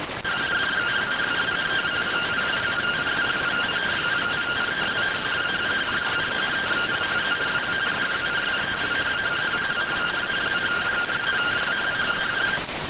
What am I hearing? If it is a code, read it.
Two-tone teleprinter signal, about 170 Hz shift.